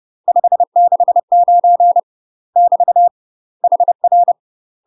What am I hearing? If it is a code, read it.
Morse code, heard as a keyed tone, about 30 wpm, 700 Hz. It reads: 569 <BT> HR
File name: UK MASTER QSOs TRAINING / 30WPM / UK MASTER QSOs 0038